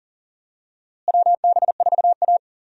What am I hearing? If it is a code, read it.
Morse code, heard as a keyed tone, about 40 wpm, 700 Hz. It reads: WB4A
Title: USA Call Signs - 4 Characters - Repeated 3x - 40wpm